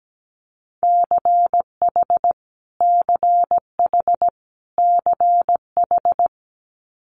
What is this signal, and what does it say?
Morse code, heard as a keyed tone, about 17 wpm, 700 Hz. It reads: CH CH CH
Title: Multi-letter Phonemes Repeated 3x 17wpm